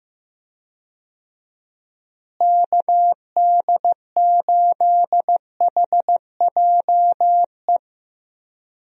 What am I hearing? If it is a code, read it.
Morse code, heard as a keyed tone, about 15 wpm, 700 Hz. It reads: KD8HJE
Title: USA Call Signs - 6 Characters - Repeated 2x - 15wpm